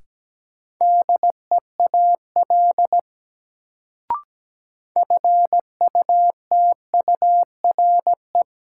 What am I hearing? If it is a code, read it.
Morse code, heard as a keyed tone, about 17 wpm, 700 Hz. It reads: DEAL FUTURE
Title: Sets of 3 Words - New Word Drill - 17wpm